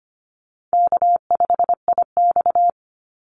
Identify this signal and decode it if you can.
Morse code, heard as a keyed tone, about 25 wpm, 700 Hz. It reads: K5IX
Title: Easy Call Signs - 25wpm